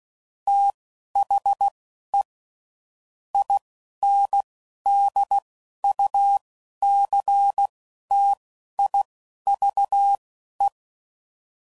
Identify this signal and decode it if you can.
Morse code, heard as a keyed tone, about 16 wpm, spaced as for 11 wpm, 800 Hz. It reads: THE INDUCTIVE
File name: CW Code Groups @10WPM / Technical Subjects / XL